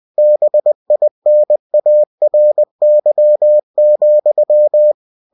Morse code, heard as a keyed tone, about 20 wpm, 600 Hz. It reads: BINARY,
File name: cw-books / winnie-the-pooh / winnie-20wpm-page-0116